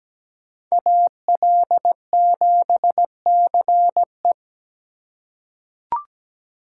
Morse code, heard as a keyed tone, about 17 wpm, 700 Hz. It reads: AL7CE
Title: USA Call Signs - 5 Characters - 3x Character Spacing - 17wpm